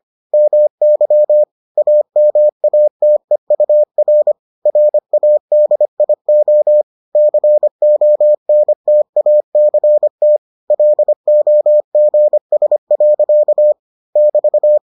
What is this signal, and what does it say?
Morse code, heard as a keyed tone, about 25 wpm, 600 Hz. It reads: MY AMATEUR RADIO CONTACT LOGS. <BT>